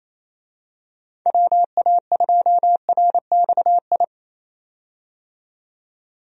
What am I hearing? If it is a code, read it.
Morse code, heard as a keyed tone, about 28 wpm, 700 Hz. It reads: WA2RXI
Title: USA Call Signs - 6 Characters - 28wpm